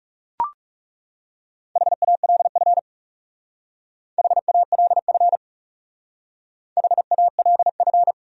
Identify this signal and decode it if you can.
Morse code, heard as a keyed tone, about 38 wpm, 700 Hz. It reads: HALF HALF HALF